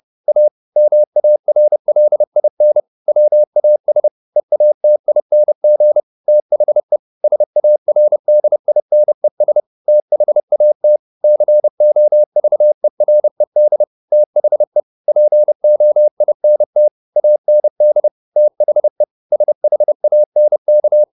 Morse code, heard as a keyed tone, about 30 wpm, 600 Hz. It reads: A MARLIN WAS EATING THE SARDINES THAT COVERED THE POINT AND THE SHANK
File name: the old man and the sea / the old man and the sea 1 30wpm